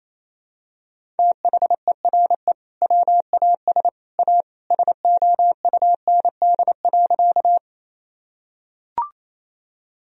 Morse code, heard as a keyed tone, about 28 wpm, 700 Hz. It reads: THERE WAS A SOUND.
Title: Sentences from Top 1000 Words - Review Part 1 - 28wpm